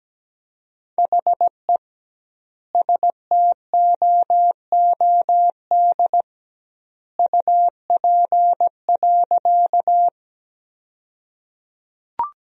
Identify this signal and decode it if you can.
Morse code, heard as a keyed tone, about 17 wpm, 700 Hz. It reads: HE STOOD UP.
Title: 2x Word-Spacing - Sentences from Top 500 Words - 17wpm